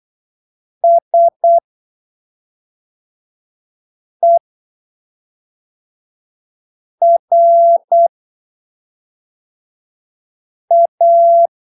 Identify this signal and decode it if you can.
Morse code, heard as a keyed tone, about 8 wpm, spaced as for 3 wpm, 680 Hz. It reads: SERA
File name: arquivos / texto / texto 25 08ppm